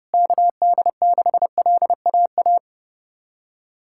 Morse code, heard as a keyed tone, about 30 wpm, 700 Hz. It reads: KD6LAA
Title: USA Callsign DE Callsign - 6 Characters - 30wpm